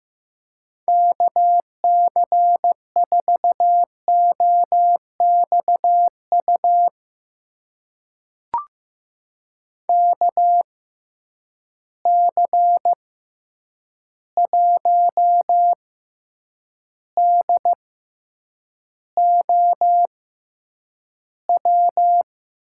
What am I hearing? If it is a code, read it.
Morse code, heard as a keyed tone, about 16 wpm, spaced as for 8 wpm, 700 Hz. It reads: KC4OXU KC1DOW
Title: USA Call Signs - 6 Characters - 6x Character Spacing - 15wpm